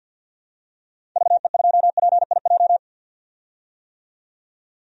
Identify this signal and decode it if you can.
Morse code, heard as a keyed tone, about 51 wpm, 700 Hz. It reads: VE2PIJ
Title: DXCC-100 Call Signs - 6 Characters - 50wpm